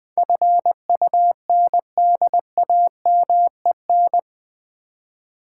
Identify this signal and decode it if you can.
Morse code, heard as a keyed tone, about 20 wpm, 700 Hz. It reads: FUNDAMEN
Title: Build A Word - 4 Letters At A Time - 11-Letter Words - 20wpm